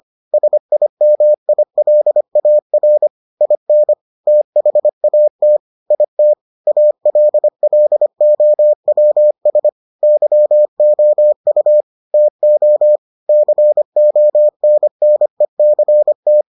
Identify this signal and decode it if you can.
Morse code, heard as a keyed tone, about 25 wpm, 600 Hz. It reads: SIMILAR IN THAT IT ALLOWS YOU TO CONNECT